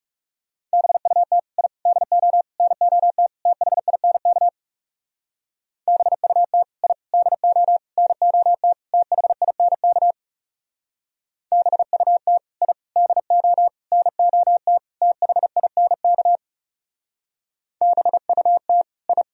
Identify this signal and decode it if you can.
Morse code, heard as a keyed tone, about 37 wpm, 700 Hz. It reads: BUT I DO NOT THINK BUT I DO NOT THINK BUT I DO NOT THINK BUT I
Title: Speed-Racing - Sets of 5 Words 30wpm